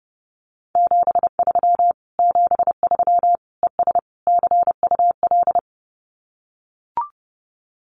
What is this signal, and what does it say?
Morse code, heard as a keyed tone, about 30 wpm, 700 Hz. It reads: 73 73 ES CUL
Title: Common QSO Phrases Encore 30wpm